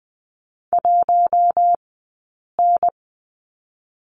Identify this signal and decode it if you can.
Morse code, heard as a keyed tone, about 20 wpm, 700 Hz. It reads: 1 N
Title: 217 Lesson 35d - Number 0 - Two Characters 20wpm